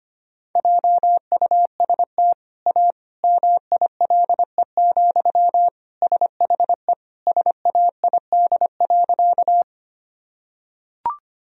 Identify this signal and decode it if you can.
Morse code, heard as a keyed tone, about 25 wpm, 700 Hz. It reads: JUST A MILE, SHE SAID.
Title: Sentences from Top 1000 Words - Review Part 1 - 25wpm